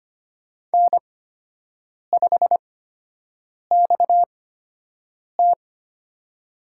Morse code, heard as a keyed tone, about 25 wpm, spaced as for 7 wpm, 700 Hz. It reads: N5XT
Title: USA Call Signs - 8x Character Spacing - 25wpm